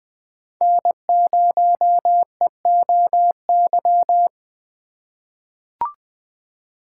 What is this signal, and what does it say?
Morse code, heard as a keyed tone, about 20 wpm, 700 Hz. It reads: N0EOY
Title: USA Call Signs - 5 Characters - 4x Character Spacing - 20wpm